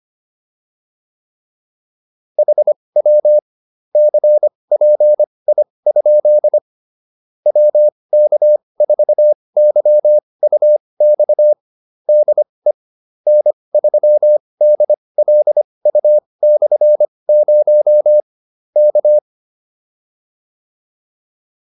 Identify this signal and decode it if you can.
Morse code, heard as a keyed tone, about 25 wpm, spaced as for 20 wpm, 600 Hz. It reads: HW CPI? WK4YUX DE N3DLU/0 K